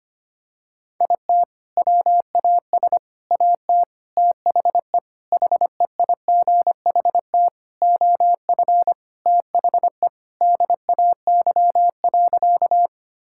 Morse code, heard as a keyed tone, about 25 wpm, 700 Hz. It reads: IT WAS AT THE HEIGHT OF THE DAY.
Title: Sentences from Top 1600 Words 25wpm